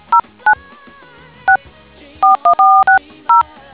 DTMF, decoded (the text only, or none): 09 6 41460